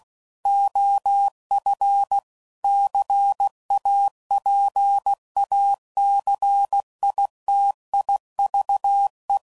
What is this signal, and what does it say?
Morse code, heard as a keyed tone, about 16 wpm, 800 Hz. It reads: OF CAPACITIVE
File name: CW Code Groups @15WPM / Technical Subjects / XC